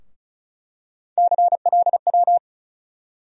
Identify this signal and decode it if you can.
Morse code, heard as a keyed tone, about 34 wpm, 700 Hz. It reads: CLW
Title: USA Call Sign Postfixes - 3 Letters 35wpm